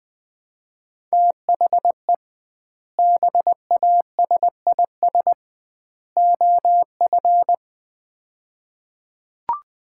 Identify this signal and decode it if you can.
Morse code, heard as a keyed tone, about 20 wpm, 700 Hz. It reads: THE BASIS OF
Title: 2x Word-Spacing - Sets of 3 Words - 20wpm